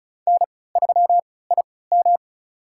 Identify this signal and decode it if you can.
Morse code, heard as a keyed tone, about 36 wpm, spaced as for 20 wpm, 700 Hz. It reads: N3IM
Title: USA Call Signs - 3x Character Spacing - 35wpm